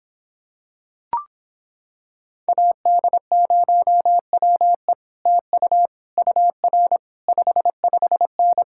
Morse code, heard as a keyed tone, about 26 wpm, 700 Hz. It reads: AD0WE TU UR 55N